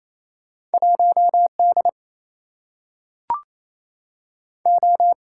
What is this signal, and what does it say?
Morse code, heard as a keyed tone, about 28 wpm, 700 Hz. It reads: 1 D O
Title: Single-Letter Single-Number 28wpm